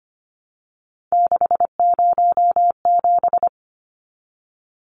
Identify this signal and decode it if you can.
Morse code, heard as a keyed tone, about 25 wpm, 700 Hz. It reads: 607
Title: Numbers - 3-Digit 25wpm